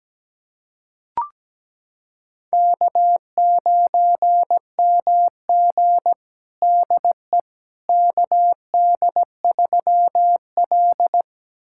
Morse code, heard as a keyed tone, about 17 wpm, 700 Hz. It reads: K9MG DE KD3L
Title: USA Callsign DE Callsign - 4 Characters - 17wpm